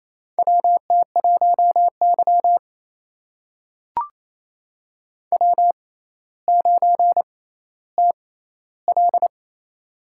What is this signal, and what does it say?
Morse code, heard as a keyed tone, about 28 wpm, spaced as for 14 wpm, 700 Hz. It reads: WT1Y W9TL
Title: USA Call Signs - 4 Characters - 6x Character Spacing - 28wpm